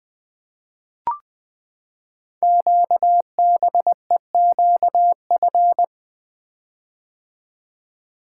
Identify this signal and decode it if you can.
Morse code, heard as a keyed tone, about 20 wpm, 700 Hz. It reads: QBEQF
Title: Code Groups - 5 Letters - 20wpm